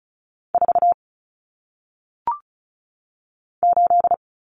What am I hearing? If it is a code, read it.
Morse code, heard as a keyed tone, about 36 wpm, 700 Hz. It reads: 4 8
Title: Numbers - 1-Digit 35wpm